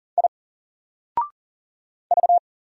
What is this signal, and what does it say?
Morse code, heard as a keyed tone, about 40 wpm, 700 Hz. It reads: I V
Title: Single Letter-Number 40wpm